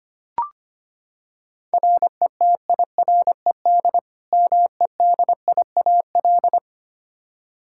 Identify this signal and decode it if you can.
Morse code, heard as a keyed tone, about 25 wpm, 700 Hz. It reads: RETIRED MEDIAL
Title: Practice QSO - Exchange 3 - More Info - 25wpm